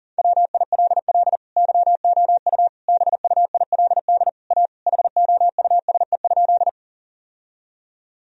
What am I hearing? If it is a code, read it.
Morse code, heard as a keyed tone, about 37 wpm, 700 Hz. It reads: WILL YOU BUILD A HOUSE?
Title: Sentences from Top 1000 Words - Review Part 1 - 40wpm